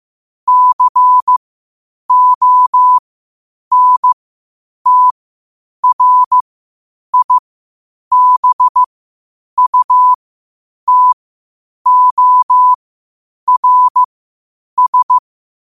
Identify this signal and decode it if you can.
Morse code, heard as a keyed tone, about 15 wpm, spaced as for 8 wpm, 1000 Hz. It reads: CONTRIBUTORS
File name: qotdfiles / 8wpm